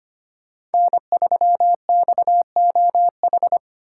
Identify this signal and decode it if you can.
Morse code, heard as a keyed tone, about 30 wpm, 700 Hz. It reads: N3XOH
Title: Speed Racing - USA 1x3 Call Signs 25wpm